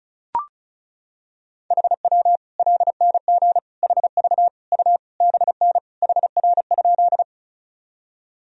Morse code, heard as a keyed tone, about 36 wpm, 700 Hz. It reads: HW LNG HV U BN HR?
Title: Sentences from Top 200 Words and Abbreviations 35wpm